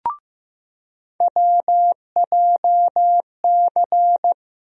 Morse code, heard as a keyed tone, about 15 wpm, 700 Hz. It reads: WJC
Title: USA Call Sign Postfixes - 3 Letters 15wpm